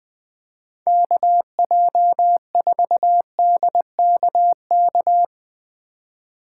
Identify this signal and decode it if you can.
Morse code, heard as a keyed tone, about 19 wpm, 700 Hz. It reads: KJ4DKK
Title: Speed Racing - USA 2x3 Call Signs 17wpm